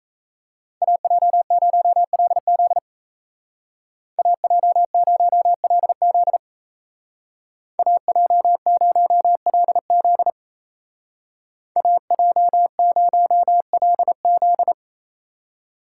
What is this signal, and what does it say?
Morse code, heard as a keyed tone, about 34 wpm, 700 Hz. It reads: AJ0LZ AJ0LZ AJ0LZ AJ0LZ
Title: Speed Racing - USA 2x2 Call Signs 28wpm